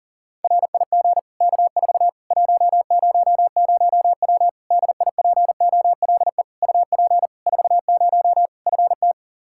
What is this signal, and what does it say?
Morse code, heard as a keyed tone, about 37 wpm, 700 Hz. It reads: RIG K4 100W DIPOLE UP 40 FT
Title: Practice QSO - Exchange 2 - Info - 40wpm